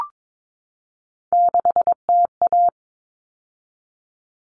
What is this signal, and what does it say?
Morse code, heard as a keyed tone, about 22 wpm, 700 Hz. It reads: 6TA
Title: Common Cut Numbers - 3-Digit 22wpm